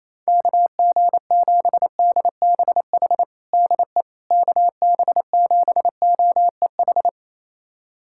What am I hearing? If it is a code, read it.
Morse code, heard as a keyed tone, about 28 wpm, 700 Hz. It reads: KG7DBH DE KB7OEH
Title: USA Callsign DE Callsign - 6 Characters - 28wpm